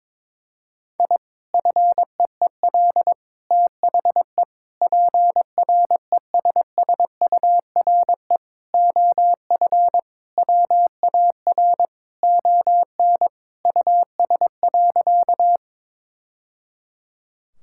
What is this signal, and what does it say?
Morse code, heard as a keyed tone, about 22 wpm, 700 Hz. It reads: I FEEL THE PRESSURE OF WAR ON US.
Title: Sentences from Top 1000 Words - Review Part 6 - 22wpm